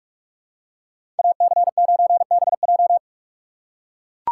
Morse code, heard as a keyed tone, about 41 wpm, 700 Hz. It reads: AC9BJ
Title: USA Call Signs - 5 Characters - 3x Character Spacing - 45wpm